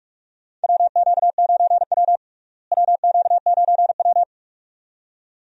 Morse code, heard as a keyed tone, about 46 wpm, 700 Hz. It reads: WQ9W WQ9W
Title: USA Call Signs - 4 Characters - Repeated 2x - 45wpm